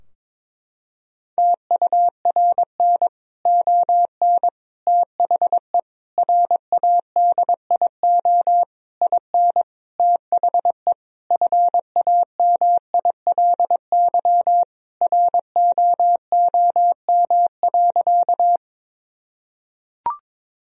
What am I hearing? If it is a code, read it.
Morse code, heard as a keyed tone, about 21 wpm, 700 Hz. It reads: TURN ON THE RADIO IN THE FAMILY ROOM.
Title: Sentences from Top 1000 Words - Review Part 3 - 22wpm